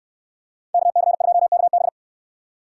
Morse code, heard as a keyed tone, about 69 wpm, 700 Hz. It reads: KC2KB